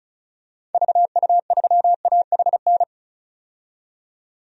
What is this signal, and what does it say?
Morse code, heard as a keyed tone, about 34 wpm, 700 Hz. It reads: VU3AHN
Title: DXCC-100 Call Signs - 6 Characters - 35wpm